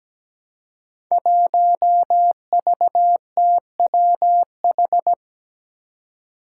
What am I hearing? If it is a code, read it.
Morse code, heard as a keyed tone, about 17 wpm, 700 Hz. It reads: 1VTWH
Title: Five Challenging Letter-Number Combinations 17wpm